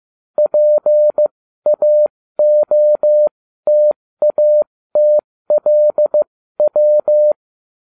Morse code, heard as a keyed tone, about 15 wpm, spaced as for 13 wpm, 600 Hz. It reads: PAOTATLW